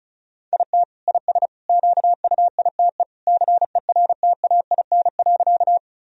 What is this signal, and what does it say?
Morse code, heard as a keyed tone, about 36 wpm, 700 Hz. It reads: IT IS QUITE CERTAIN.
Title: Sentences from Top 1000 Words - Review Part 2 - 35wpm